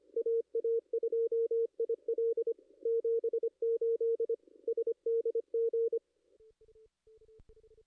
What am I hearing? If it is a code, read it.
Morse code, heard as a keyed tone, about 24 wpm, 440 Hz. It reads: AA2IL 78 SDG
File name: contests / ss-cw-k5tr-op-n5rz -2300